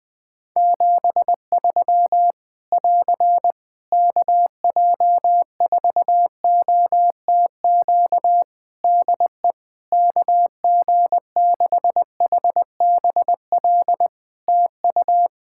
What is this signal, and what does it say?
Morse code, heard as a keyed tone, about 21 wpm, 700 Hz. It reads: 73 <AR> KJ4OTQ DE KG6HBL TU
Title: Practice QSO - Exchange 4 - Ending - 20wpm